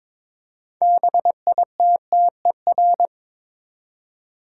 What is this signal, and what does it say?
Morse code, heard as a keyed tone, about 22 wpm, 700 Hz. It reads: BITTER
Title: Six Letter Words - 22wpm